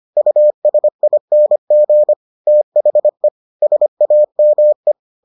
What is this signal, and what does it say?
Morse code, heard as a keyed tone, about 25 wpm, 600 Hz. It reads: USING THE SAME